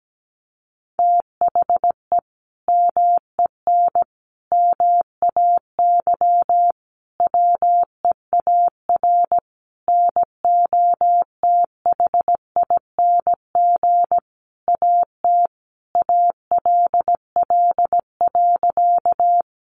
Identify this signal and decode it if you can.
Morse code, heard as a keyed tone, about 17 wpm, 700 Hz. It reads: THE MEN MAY WEAR NOTHING AT ALL.
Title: Sentences from Top 1500 Words 17wpm